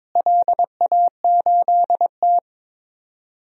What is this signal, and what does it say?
Morse code, heard as a keyed tone, about 22 wpm, 700 Hz. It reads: LA8T
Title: DXCC-100 Call Signs - 4 Characters - Repeated 3x - 22wpm